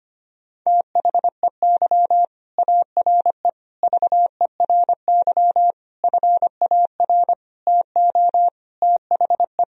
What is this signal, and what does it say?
Morse code, heard as a keyed tone, about 25 wpm, 700 Hz. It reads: THEY ARE VERY FAR TO THE
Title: Sentences from Top 1000 Words - Review Part 2 - 25wpm